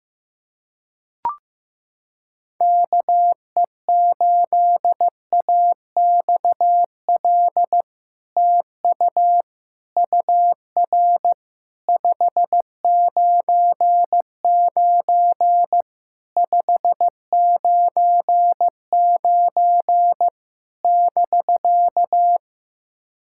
Morse code, heard as a keyed tone, about 15 wpm, 700 Hz. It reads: KE8AXL TU UR 599 599 <BK>
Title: SOTA QSO Practice 15wpm